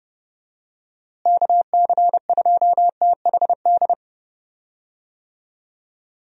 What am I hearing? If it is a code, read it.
Morse code, heard as a keyed tone, about 30 wpm, 700 Hz. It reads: KC2THD
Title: USA Call Signs - 6 Characters - Repeated 3x - 30wpm